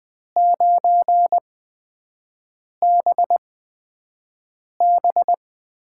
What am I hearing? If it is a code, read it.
Morse code, heard as a keyed tone, about 20 wpm, spaced as for 5 wpm, 700 Hz. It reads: 9BB
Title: USA Call Signs - 5 Characters - 8x Character Spacing - 20wpm